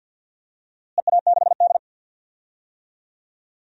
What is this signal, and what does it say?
Morse code, heard as a keyed tone, about 47 wpm, 700 Hz. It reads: EA6D
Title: DXCC-100 Call Signs - 4 Characters - Repeated 3x - 50wpm